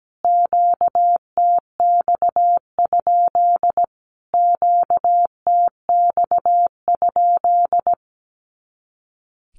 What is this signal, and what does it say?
Morse code, heard as a keyed tone, about 17 wpm, 700 Hz. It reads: QTX? QTX?
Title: Q-Codes Repeated 3x 17wpm